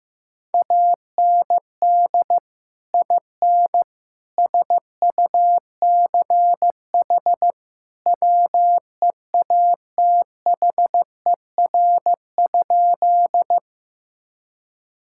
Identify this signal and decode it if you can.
Morse code, heard as a keyed tone, about 15 wpm, 700 Hz. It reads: AND IN SUCH WEATHER?
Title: Sentences from Top 1000 Words - Review Part 5 - 15wpm